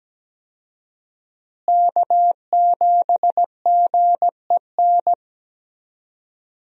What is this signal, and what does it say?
Morse code, heard as a keyed tone, about 17 wpm, 700 Hz. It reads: K7GEN
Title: USA Call Signs - 5 Characters - Repeated 3x - 17wpm